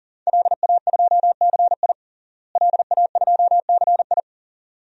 Morse code, heard as a keyed tone, about 37 wpm, 700 Hz. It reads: LA2CI LA2CI
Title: DXCC-100 Call Signs - 5 Characters - Repeated 2x - 40wpm